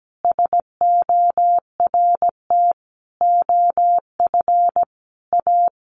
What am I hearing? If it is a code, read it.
Morse code, heard as a keyed tone, about 17 wpm, 700 Hz. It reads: SORT OF A
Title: Sentences from Top 1300 Words 17wpm